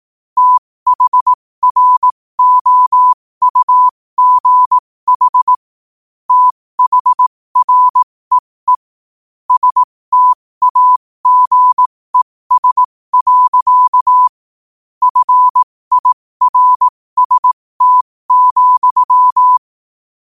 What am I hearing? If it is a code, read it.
Morse code, heard as a keyed tone, about 18 wpm, spaced as for 15 wpm, 1000 Hz. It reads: THROUGH THREE STAGES. FIRST,
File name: qotdfiles / 15wpm